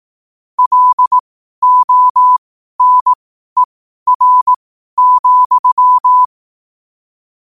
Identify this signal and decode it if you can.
Morse code, heard as a keyed tone, about 18 wpm, spaced as for 12 wpm, 1000 Hz. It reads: LONER,
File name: qotdfiles / 12wpm